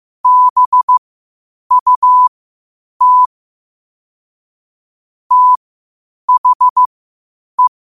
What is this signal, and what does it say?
Morse code, heard as a keyed tone, about 15 wpm, spaced as for 8 wpm, 1000 Hz. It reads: BUT THE
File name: qotdfiles / 8wpm